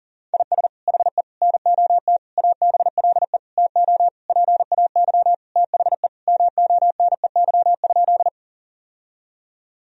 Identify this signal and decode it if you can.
Morse code, heard as a keyed tone, about 37 wpm, 700 Hz. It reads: IS HE NOT ABLE TO PAY THE MONEY?
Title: Sentences from Top 1000 Words - Review Part 2 - 40wpm